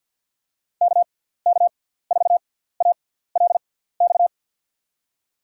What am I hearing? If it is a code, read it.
Morse code, heard as a keyed tone, about 53 wpm, spaced as for 17 wpm, 700 Hz. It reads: KK4ALX
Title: USA Call Signs - 6 Characters - 6x Character Spacing - 50wpm